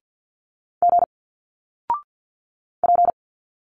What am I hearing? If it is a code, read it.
Morse code, heard as a keyed tone, about 50 wpm, 700 Hz. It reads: G P
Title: Single Letters 50wpm